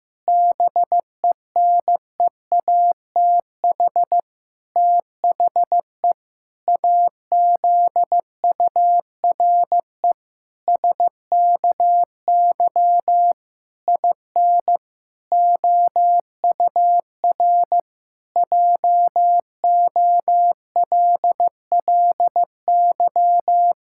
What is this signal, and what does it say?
Morse code, heard as a keyed tone, about 15 wpm, 700 Hz. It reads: BENEATH THE AZURE SKY IN OUR JOLLY
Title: Pangram Challenge v2 - 15wpm